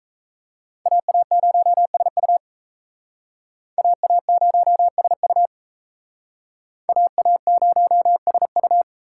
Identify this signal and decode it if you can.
Morse code, heard as a keyed tone, about 35 wpm, 700 Hz. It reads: AA0SU AA0SU AA0SU
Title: Speed Racing - USA 2x2 Call Signs 28wpm